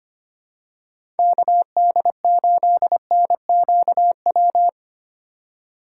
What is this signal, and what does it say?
Morse code, heard as a keyed tone, about 25 wpm, 700 Hz. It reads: KD8NQW
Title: USA Call Signs - 6 Characters - 4x Character Spacing - 25wpm